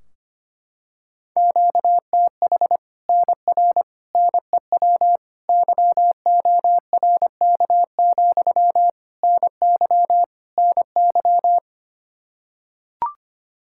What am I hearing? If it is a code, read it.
Morse code, heard as a keyed tone, about 25 wpm, 700 Hz. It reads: QTH NR NEW YORK, NY NY
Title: Practice QSO - Exchange 1 - Beginning - 25wpm